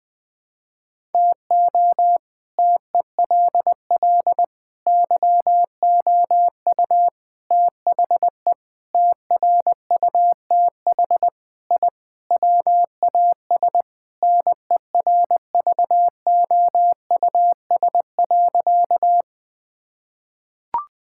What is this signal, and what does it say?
Morse code, heard as a keyed tone, about 20 wpm, 700 Hz. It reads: TO TELL YOU THE TRUTH I WAS NERVOUS.
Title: Sentences with 5-Word Phrases - 20wpm